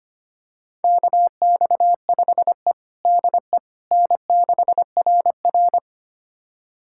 Morse code, heard as a keyed tone, about 25 wpm, 700 Hz. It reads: KX5E DE N6RR
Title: USA Callsign DE Callsign - 4 Characters - 25wpm